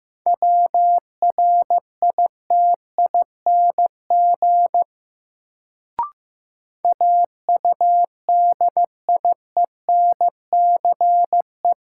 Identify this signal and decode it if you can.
Morse code, heard as a keyed tone, about 15 wpm, 700 Hz. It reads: WRITING AUDIENCE
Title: Sentences from Top 800 Words - New Word Drill - 15wpm